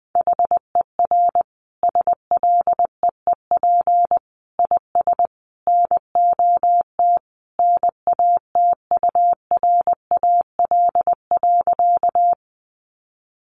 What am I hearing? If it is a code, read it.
Morse code, heard as a keyed tone, about 20 wpm, 700 Hz. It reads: HER SLEEP IS NOT NATURAL.
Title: Sentences from Top 700 Words 20wpm